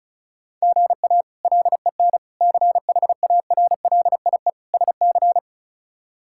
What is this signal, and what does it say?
Morse code, heard as a keyed tone, about 34 wpm, 700 Hz. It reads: GA LEN CHARLIE SC
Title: K1USN SST QSO Practice - 35wpm